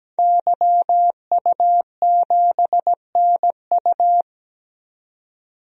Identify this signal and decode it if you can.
Morse code, heard as a keyed tone, about 17 wpm, 700 Hz. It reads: YU7NU
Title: DXCC-100 Call Signs - 5 Characters - 17wpm